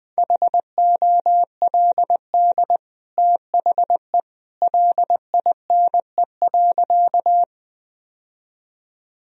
Morse code, heard as a keyed tone, about 20 wpm, 700 Hz. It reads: HOLD THE LINE.
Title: Sentences from Top 1000 Words - Review Part 2 - 20wpm